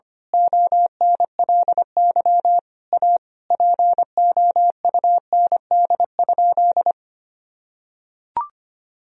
Morse code, heard as a keyed tone, about 25 wpm, 700 Hz. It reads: ONLY A POUND?
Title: Sentences from Top 1000 Words - Review Part 2 - 25wpm